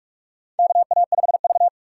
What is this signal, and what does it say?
Morse code, heard as a keyed tone, about 46 wpm, 700 Hz. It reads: KA5V
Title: USA Call Signs - 4 Characters - 4x Character Spacing - 45wpm